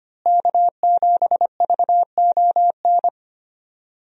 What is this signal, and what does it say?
Morse code, heard as a keyed tone, about 25 wpm, 700 Hz. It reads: K7VON
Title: USA Call Signs - 5 Characters - Repeated 2x - 25wpm